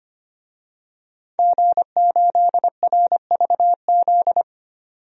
Morse code, heard as a keyed tone, about 25 wpm, 700 Hz. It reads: G8RVZ
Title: DXCC-100 Call Signs - 5 Characters - 25wpm